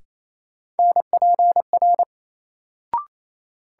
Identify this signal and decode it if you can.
Morse code, heard as a keyed tone, about 27 wpm, 700 Hz. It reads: NPR
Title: Sets of 5 Words - New Word Drill - 28wpm